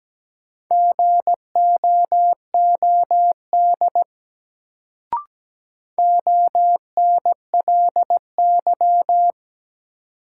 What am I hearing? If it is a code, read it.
Morse code, heard as a keyed tone, about 17 wpm, 700 Hz. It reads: GOOD ONLY
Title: Sets of 2 Words - New Word Drill - 17wpm